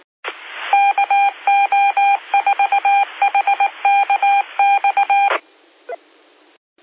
Morse code, heard as a keyed tone, about 19 wpm, 800 Hz. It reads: KO4HKX
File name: qsorec SimplexLogic